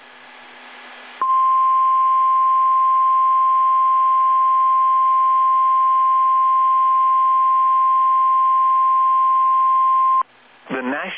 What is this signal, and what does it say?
Emergency Alert System (SAME) tones (the 1050 Hz attention tone), with speech. It is heard from 1.2 to 10.3 s.